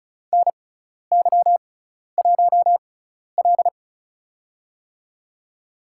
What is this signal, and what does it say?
Morse code, heard as a keyed tone, about 34 wpm, spaced as for 12 wpm, 700 Hz. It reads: NY1L
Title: USA Call Signs - 4 Characters - 6x Character Spacing - 35wpm